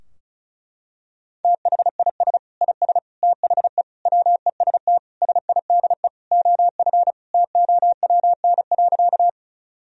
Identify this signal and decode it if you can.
Morse code, heard as a keyed tone, about 34 wpm, 700 Hz. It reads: THIS IS THE WEST SIDE OF TOWN.
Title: Sentences from Top 1000 Words - Review Part 2 - 35wpm